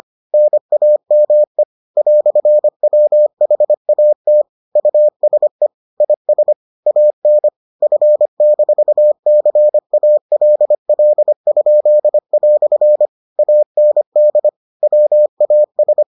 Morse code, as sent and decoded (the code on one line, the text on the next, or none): -. .- -- . / .-..-. .-- .... .- - / ..- ... / .. ... / .- -. / ..-. -....- -.-. .- .-.. .-.. ..--.. .-..-. / .- -. -.. / .-- .- ...
NAME "WHAT USE IS AN F-CALL?" AND WAS